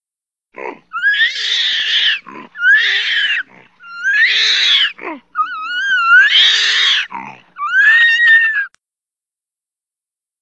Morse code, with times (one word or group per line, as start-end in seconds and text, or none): none